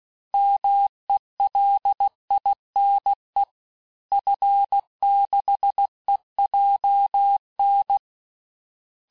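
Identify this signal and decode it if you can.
Morse code, heard as a keyed tone, about 16 wpm, 800 Hz. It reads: MELINE F6EJN
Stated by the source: Morse course text, end of QSO, sent at 15 words per minute.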